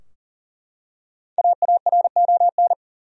Morse code, heard as a keyed tone, about 40 wpm, 700 Hz. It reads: AARON
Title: USA Common Names - Max 5 Letters - 40wpm